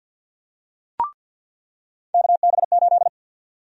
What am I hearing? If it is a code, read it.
Morse code, heard as a keyed tone, about 50 wpm, 700 Hz. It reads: KB8
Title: USA Call Sign Prefixes - 50wpm